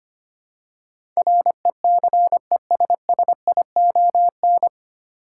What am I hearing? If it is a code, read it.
Morse code, heard as a keyed tone, about 25 wpm, 700 Hz. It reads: RECESSION
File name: practice / 25wpm / 9 Letter Words - First 3 Letters Given - 25wpm